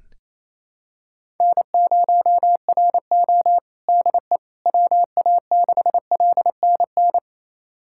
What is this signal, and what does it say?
Morse code, heard as a keyed tone, about 28 wpm, 700 Hz. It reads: N0RO DE WA6LNN
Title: USA Callsign DE Callsign 28wpm